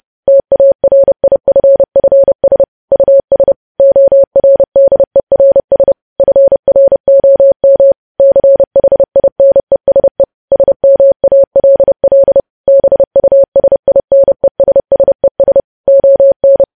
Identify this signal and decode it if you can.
Morse code, heard as a keyed tone, about 30 wpm, 550 Hz. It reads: TARIFFS US ORDERS FROM CHINESE SMALL BUSINESSES ON
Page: news headlines in morse code at 30 wpm